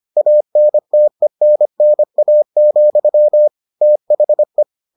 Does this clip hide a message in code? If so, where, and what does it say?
Yes: Morse code from 0.2 to 4.6 s, heard as a keyed tone, about 25 wpm, 600 Hz. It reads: ANTENNA, THE